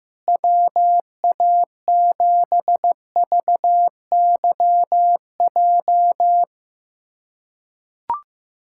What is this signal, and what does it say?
Morse code, heard as a keyed tone, about 16 wpm, 700 Hz. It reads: WA7VYJ
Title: USA Callsign DE Callsign 15wpm